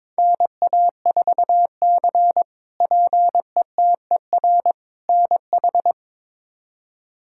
Morse code, heard as a keyed tone, about 22 wpm, 700 Hz. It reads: NA4C PETER NH
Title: K1USN SST QSO Practice - 22wpm